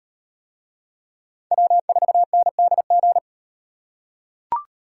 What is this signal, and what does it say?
Morse code, heard as a keyed tone, about 38 wpm, 700 Hz. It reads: W4NDG